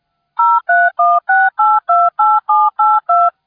DTMF, heard as telephone touch tones, tones of about 220 ms. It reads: 0316828782